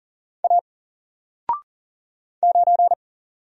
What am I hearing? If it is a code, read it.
Morse code, heard as a keyed tone, about 40 wpm, 700 Hz. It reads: A 9
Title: Single Letter-Number Alt 40wpm